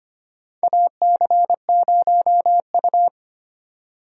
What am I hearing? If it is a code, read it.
Morse code, heard as a keyed tone, about 25 wpm, 700 Hz. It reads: AC0U
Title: USA Call Signs - 4 Characters - 25wpm